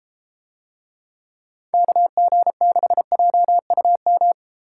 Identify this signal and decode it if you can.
Morse code, heard as a keyed tone, about 31 wpm, 700 Hz. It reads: KG6JUM